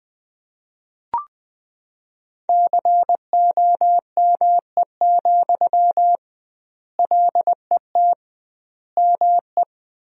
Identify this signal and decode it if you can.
Morse code, heard as a keyed tone, about 20 wpm, 700 Hz. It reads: COME, LET ME
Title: 2x Word-Spacing - Sentences from Top 500 Words - 20wpm